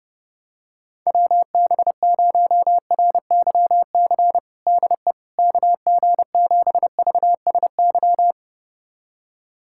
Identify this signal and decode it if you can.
Morse code, heard as a keyed tone, about 30 wpm, 700 Hz. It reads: WB0RYC DE KG7VSY
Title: USA Callsign DE Callsign 30wpm